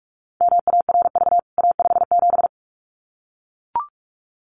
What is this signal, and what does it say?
Morse code, heard as a keyed tone, about 44 wpm, 700 Hz. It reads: MARV A57
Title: CWT Exchange 45wpm